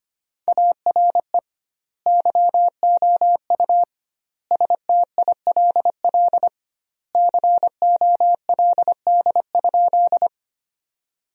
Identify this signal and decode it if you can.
Morse code, heard as a keyed tone, about 25 wpm, 700 Hz. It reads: ARE YOU STILL COLD?
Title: 2x Word-Spacing - Sentences from Top 400 Words - 25wpm